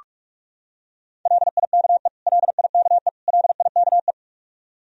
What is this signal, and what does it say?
Morse code, heard as a keyed tone, about 41 wpm, 700 Hz. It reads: LIKE LIKE LIKE
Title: Top 100 Words Repeated 45wpm